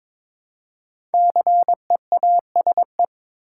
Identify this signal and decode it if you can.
Morse code, heard as a keyed tone, about 22 wpm, 700 Hz. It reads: CEASE
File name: 5 Letter Words - First 3 Letters Given - 22wpm